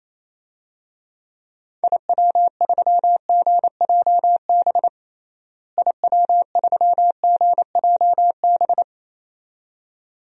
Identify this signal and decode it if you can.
Morse code, heard as a keyed tone, about 28 wpm, 700 Hz. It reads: IW3GJB IW3GJB
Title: DXCC-100 Call Signs - 6 Characters - Repeated 2x - 28wpm